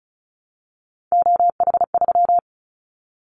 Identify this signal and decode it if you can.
Morse code, heard as a keyed tone, about 34 wpm, 700 Hz. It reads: OH3
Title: Callsign Prefixes for Top 25 DXCC - Repeated 3x - 35wpm